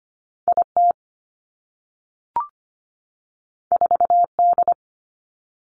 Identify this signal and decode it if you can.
Morse code, heard as a keyed tone, about 25 wpm, 700 Hz. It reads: I T 4 D
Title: Two Challenging Letter-Number Combinations 25wpm